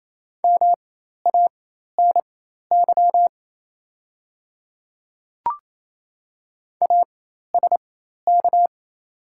Morse code, heard as a keyed tone, about 28 wpm, spaced as for 13 wpm, 700 Hz. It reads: MANY ASK
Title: 4x Character Spacing - Top 200 Words - 28wpm